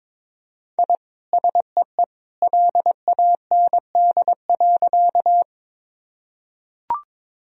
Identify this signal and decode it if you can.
Morse code, heard as a keyed tone, about 22 wpm, 700 Hz. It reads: I SEE LAND.
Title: Sentences from Top 1000 Words - Review Part 1 - 22wpm